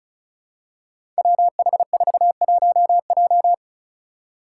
Morse code, heard as a keyed tone, about 36 wpm, 700 Hz. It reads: WH41J
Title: Five Challenging Letter-Number Combinations 35wpm